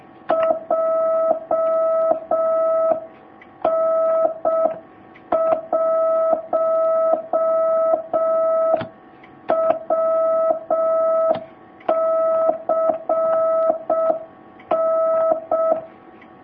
Morse code, heard as a keyed tone, about 6 wpm, 650 Hz. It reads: JN1WCN
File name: jh8cbh / 2019cw